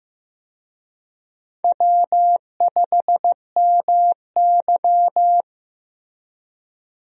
Morse code, heard as a keyed tone, about 15 wpm, 700 Hz. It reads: W5MY
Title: USA Call Signs - 4 Characters - Repeated 3x - 15wpm